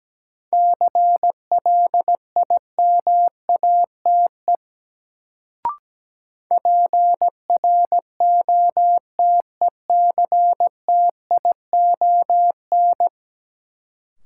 Morse code, heard as a keyed tone, about 17 wpm, 700 Hz. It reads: CLIMATE PROTECTION
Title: Sentences from Top 1600 Words - New Word Drill - 17wpm